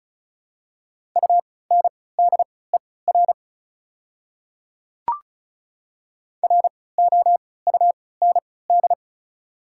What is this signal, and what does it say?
Morse code, heard as a keyed tone, about 36 wpm, spaced as for 20 wpm, 700 Hz. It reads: UNDER ROUND
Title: 3x Character Spacing - Top 200 Words - 35wpm